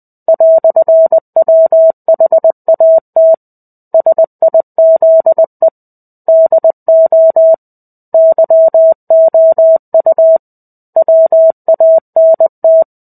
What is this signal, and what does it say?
Morse code, heard as a keyed tone, about 20 wpm, 650 Hz. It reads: "WHAT SIZE DO YOU WANT
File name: 05-Chapter-05 20 WPM